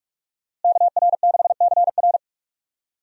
Morse code, heard as a keyed tone, about 41 wpm, 700 Hz. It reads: KR6CR
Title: USA Call Signs - 5 Characters - Repeated 3x - 45wpm